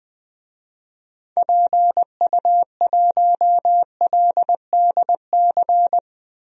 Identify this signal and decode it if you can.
Morse code, heard as a keyed tone, about 20 wpm, 700 Hz. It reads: PU1LDC
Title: DXCC-100 Call Signs - 6 Characters - Repeated 2x - 20wpm